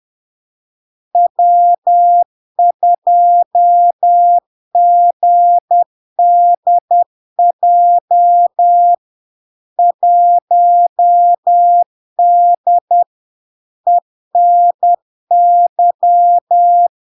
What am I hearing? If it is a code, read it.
Morse code, heard as a keyed tone, about 10 wpm, 700 Hz. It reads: W2GDJ 1D ENY
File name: LARK-FD-2022-Log-in-CW-10wpm-700-Hz